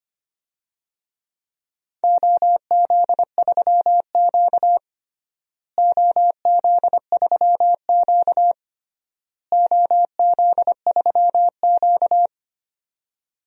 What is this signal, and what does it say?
Morse code, heard as a keyed tone, about 25 wpm, 700 Hz. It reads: OZ3Q OZ3Q OZ3Q
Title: DXCC-100 Call Signs - 4 Characters - Repeated 3x - 25wpm